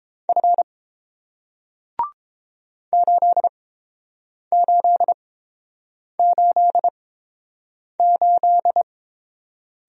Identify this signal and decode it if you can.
Morse code, heard as a keyed tone, about 30 wpm, spaced as for 7 wpm, 690 Hz. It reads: F 8888